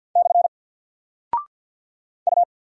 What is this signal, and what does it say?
Morse code, heard as a keyed tone, about 50 wpm, 700 Hz. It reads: / U
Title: Single Character (Letter:Number:Punctuation:Prosign) 50wpm